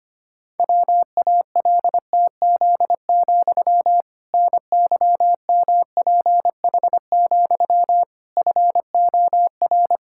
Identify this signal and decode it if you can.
Morse code, heard as a keyed tone, about 25 wpm, 700 Hz. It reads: WALTZ, NYMPH, FOR
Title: Pangram Challenge v2 - 25wpm